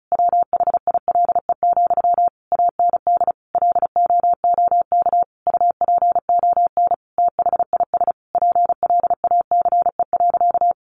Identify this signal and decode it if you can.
Morse code, heard as a keyed tone, about 34 wpm, 700 Hz. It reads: WHILE, AND LOOK UPON THIS PLACE.
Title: Sentences from Top 1100 Words 35wpm